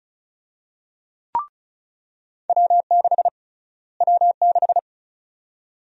Morse code, heard as a keyed tone, about 34 wpm, 700 Hz. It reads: W6 W6
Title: USA Call Sign Prefixes - Repeated 2x - 35wpm